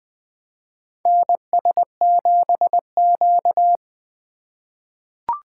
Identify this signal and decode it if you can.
Morse code, heard as a keyed tone, about 20 wpm, 700 Hz. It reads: NS7Q
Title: USA Call Signs - 2x Character Spacing - 20wpm